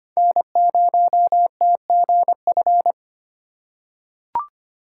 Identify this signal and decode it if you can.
Morse code, heard as a keyed tone, about 25 wpm, 700 Hz. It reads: N0TGF
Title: USA Call Signs - 4x Character Spacing - 25wpm